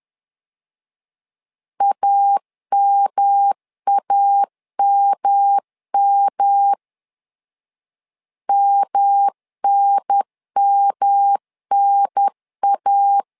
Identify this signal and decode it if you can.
Morse code, heard as a keyed tone, about 11 wpm, 800 Hz. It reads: AMAMM MNMNA
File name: LEZIONI / 3 MWRO / 3 M 10wpm 1828